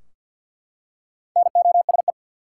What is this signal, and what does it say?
Morse code, heard as a keyed tone, about 47 wpm, 700 Hz. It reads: NOSE
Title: Four Letter Words - 50wpm